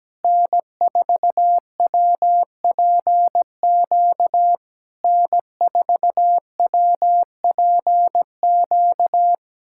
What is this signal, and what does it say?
Morse code, heard as a keyed tone, about 17 wpm, 700 Hz. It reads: N4WPQ N4WPQ
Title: SOTA QSO Practice 17wpm